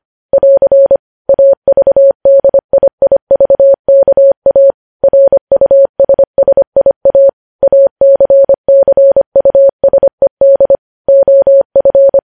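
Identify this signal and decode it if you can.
Morse code, heard as a keyed tone, about 25 wpm, 550 Hz. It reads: <AR> AVDIIVKA RUSSIA ACCUSED OF
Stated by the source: Morse news headlines sent at 25 wpm.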